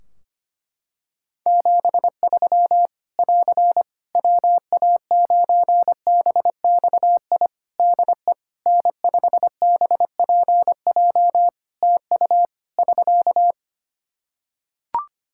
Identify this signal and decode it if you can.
Morse code, heard as a keyed tone, about 25 wpm, 700 Hz. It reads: 73 <AR> WA9BXI DE N5BPJ TU <SK>
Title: K1USN SST QSO Practice 25wpm